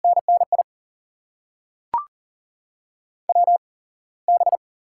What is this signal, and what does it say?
Morse code, heard as a keyed tone, about 38 wpm, spaced as for 9 wpm, 700 Hz. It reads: NNIWB
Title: USA Call Signs - 8x Character Spacing - 40wpm